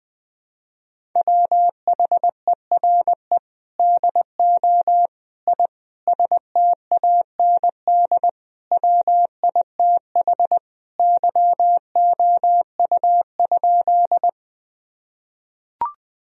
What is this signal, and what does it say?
Morse code, heard as a keyed tone, about 21 wpm, 700 Hz. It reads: WHERE DO I STAND WITH YOU?
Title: Sentences from Top 1000 Words - Review Part 1 - 20wpm